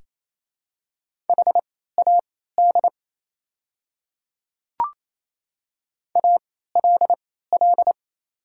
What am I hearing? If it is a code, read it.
Morse code, heard as a keyed tone, about 28 wpm, spaced as for 16 wpm, 700 Hz. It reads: HAD ALL
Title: 3x Character Spacing - Top 100 Words - 28wpm